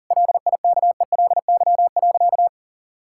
Morse code, heard as a keyed tone, about 37 wpm, 700 Hz. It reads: LIKELY.
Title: Sentences from Top 1600 Words 40wpm